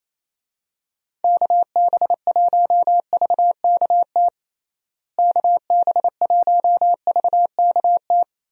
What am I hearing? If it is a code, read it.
Morse code, heard as a keyed tone, about 28 wpm, 700 Hz. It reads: KB1VKT KB1VKT
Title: USA Call Signs - 6 Characters - Repeated 3x - 28wpm